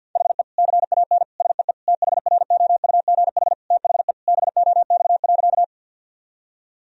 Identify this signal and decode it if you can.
Morse code, heard as a keyed tone, about 50 wpm, 700 Hz. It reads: HE CAN SEE THROUGH THE BOX.
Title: Sentences from Top 1000 Words - Review Part 2 - 50wpm